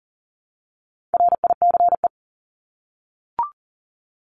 Morse code, heard as a keyed tone, about 40 wpm, 700 Hz. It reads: RICE
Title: Top 1501-1750 Words 40wpm